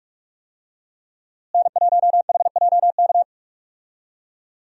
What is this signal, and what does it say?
Morse code, heard as a keyed tone, about 43 wpm, 700 Hz. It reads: N1HJK
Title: USA Call Signs - 5 Characters - Repeated 2x - 45wpm